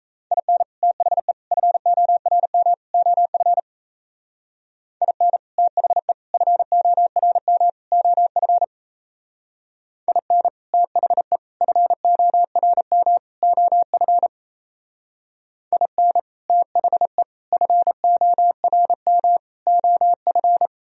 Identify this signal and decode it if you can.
Morse code, heard as a keyed tone, about 35 wpm, 700 Hz. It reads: IN THE FORM OF IN THE FORM OF IN THE FORM OF IN THE FORM OF
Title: Speed-Racing - Sets of 4 Words 28wpm